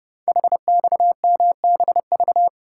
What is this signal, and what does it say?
Morse code, heard as a keyed tone, about 30 wpm, 690 Hz. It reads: HXMBV
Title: Code Groups - 5 Letters - 30wpm